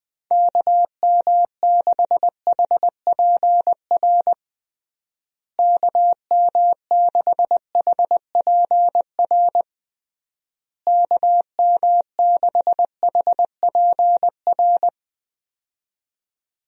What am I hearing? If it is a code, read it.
Morse code, heard as a keyed tone, about 20 wpm, 700 Hz. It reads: KM6HPR KM6HPR KM6HPR
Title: USA Call Signs - 6 Characters - Repeated 3x - 20wpm